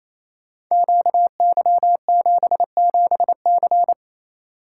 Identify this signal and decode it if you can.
Morse code, heard as a keyed tone, about 29 wpm, 700 Hz. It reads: QY77C
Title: Code Groups - 5 Letter:Numbers - 28wpm